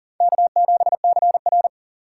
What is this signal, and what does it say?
Morse code, heard as a keyed tone, about 40 wpm, 700 Hz. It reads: K7CR
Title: USA Call Signs - 4 Characters - 6x Character Spacing - 40wpm